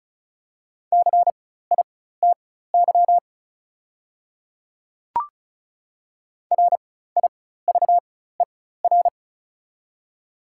Morse code, heard as a keyed tone, about 36 wpm, spaced as for 16 wpm, 700 Hz. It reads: CITY RIVER
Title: 4x Character Spacing - Top 500 Words - 35wpm